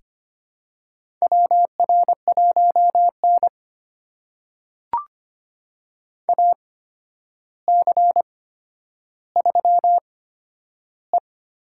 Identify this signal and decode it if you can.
Morse code, heard as a keyed tone, about 25 wpm, spaced as for 11 wpm, 700 Hz. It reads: WR1N AC3E
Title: USA Call Signs - 4 Characters - 8x Character Spacing - 25wpm